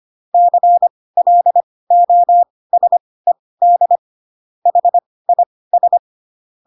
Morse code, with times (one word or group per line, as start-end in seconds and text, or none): 0.3-4.0 CLOSED
4.7-6.0 HIS